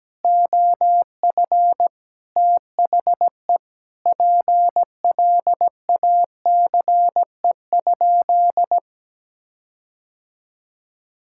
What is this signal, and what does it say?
Morse code, heard as a keyed tone, about 17 wpm, 700 Hz. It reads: OF THE PLACE?
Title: Sentences from Top 1600 Words 17wpm